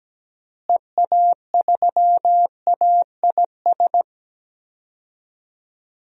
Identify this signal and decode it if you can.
Morse code, heard as a keyed tone, about 17 wpm, 700 Hz. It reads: EA3AIS
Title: DXCC-100 Call Signs - 6 Characters - Repeated 2x - 17wpm